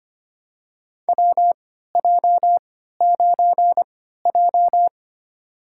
Morse code, heard as a keyed tone, about 25 wpm, spaced as for 14 wpm, 700 Hz. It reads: WJ9J
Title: USA Call Signs - 4 Characters - 3x Character Spacing - 25wpm